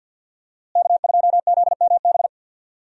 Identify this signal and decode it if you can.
Morse code, heard as a keyed tone, about 53 wpm, 700 Hz. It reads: K2ZMB
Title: USA Call Signs - 5 Characters - Repeated 3x - 50wpm